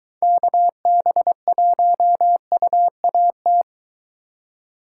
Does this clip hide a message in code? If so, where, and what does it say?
Yes: Morse code from 0.2 to 3.6 s, heard as a keyed tone, about 23 wpm, 700 Hz. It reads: KB1UAT